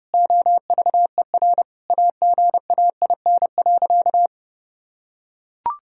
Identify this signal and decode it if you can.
Morse code, heard as a keyed tone, about 30 wpm, 700 Hz. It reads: OVER AGAIN.
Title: Sentences from Top 1000 Words - Review Part 1 - 30wpm